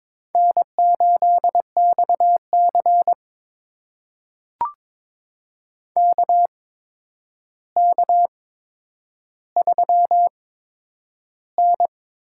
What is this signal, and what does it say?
Morse code, heard as a keyed tone, about 23 wpm, spaced as for 10 wpm, 700 Hz. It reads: N8XC KK3N
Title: USA Call Signs - 4 Characters - 8x Character Spacing - 22wpm